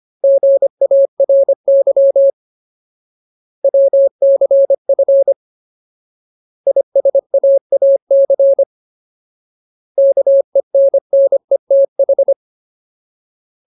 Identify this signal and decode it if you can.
Morse code, heard as a keyed tone, about 25 wpm, 550 Hz. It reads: GARY WCF ISAAC KENNETH
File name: Ultimate Word List / 25WPM Ultimate